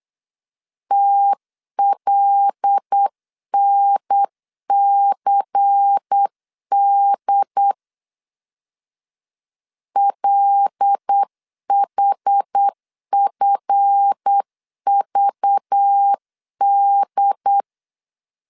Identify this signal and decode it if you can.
Morse code, heard as a keyed tone, about 8 wpm, 800 Hz. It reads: TLNCD LHFVD
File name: LEZIONI / 7 GF9Z / 7 GF 8wpm 3108